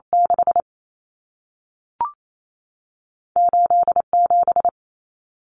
Morse code, heard as a keyed tone, about 28 wpm, 700 Hz. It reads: 6 8 7
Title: Common Cut Numbers - 2-Digit 28wpm